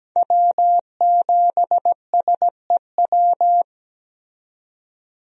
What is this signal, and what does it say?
Morse code, heard as a keyed tone, about 17 wpm, 700 Hz. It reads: W7SEW
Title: USA Call Signs - 5 Characters - 17wpm